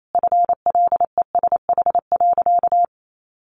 Morse code, heard as a keyed tone, about 28 wpm, 700 Hz. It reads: FLESH.
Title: A Sound Of Thunder 28wpm